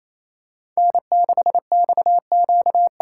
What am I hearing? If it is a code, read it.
Morse code, heard as a keyed tone, about 28 wpm, 700 Hz. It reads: N6XQ
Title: USA Callsign DE Callsign - 5 Characters - 28wpm